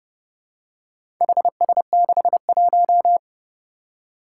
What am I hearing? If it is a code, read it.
Morse code, heard as a keyed tone, about 30 wpm, 700 Hz. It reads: HS61
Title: Four Challenging Letter-Number Combinations 30wpm